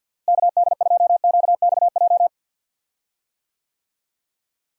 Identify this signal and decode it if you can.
Morse code, heard as a keyed tone, about 46 wpm, 700 Hz. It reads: KD2QXJ
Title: USA Call Signs - 6 Characters - Repeated 2x - 50wpm